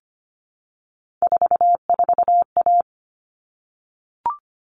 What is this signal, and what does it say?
Morse code, heard as a keyed tone, about 25 wpm, 700 Hz. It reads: 44A
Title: All Cut Numbers - 3-Digit 25wpm